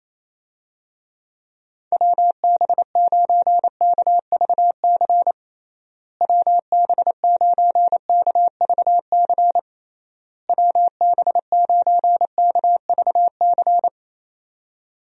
Morse code, heard as a keyed tone, about 28 wpm, 700 Hz. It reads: WB9KVC WB9KVC WB9KVC
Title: USA Call Signs - 6 Characters - Repeated 3x - 28wpm